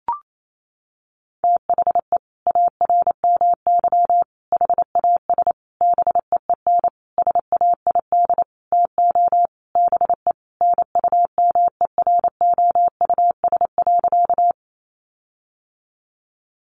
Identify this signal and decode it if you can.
Morse code, heard as a keyed tone, about 28 wpm, 700 Hz. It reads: THE ARMY HAS BEEN SAID TO BE NUMEROUS.
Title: Sentences from Top 1400 Words 28wpm